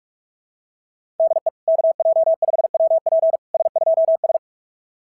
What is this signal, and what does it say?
Morse code, heard as a keyed tone, about 44 wpm, 660 Hz. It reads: DE KJ5WP S2S
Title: SOTA QSO Practice 45wpm